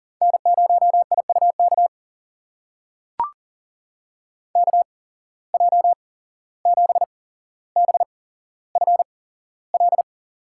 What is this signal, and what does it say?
Morse code, heard as a keyed tone, about 40 wpm, spaced as for 16 wpm, 700 Hz. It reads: N0IUK KJ7BFL
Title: USA Call Signs - 8x Character Spacing - 40wpm